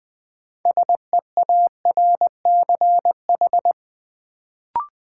Morse code, heard as a keyed tone, about 21 wpm, 700 Hz. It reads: SEARCH
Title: Sentences from Top 800 Words - New Word Drill - 20wpm